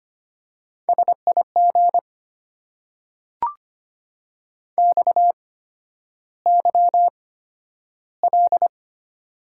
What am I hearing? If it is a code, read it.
Morse code, heard as a keyed tone, about 25 wpm, spaced as for 11 wpm, 700 Hz. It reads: SIG XYL
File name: Top 90 QSO Elements - 8x Character Spacing - 25wpm